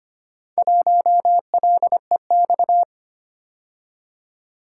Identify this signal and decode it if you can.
Morse code, heard as a keyed tone, about 25 wpm, 700 Hz. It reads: 1LEX
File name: practice / 25wpm / Four Letter-Number Combinations - 25wpm